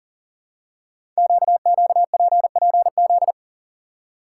Morse code, heard as a keyed tone, about 41 wpm, 700 Hz. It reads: QQPPZ
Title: Code Groups - 5 Letters - 40wpm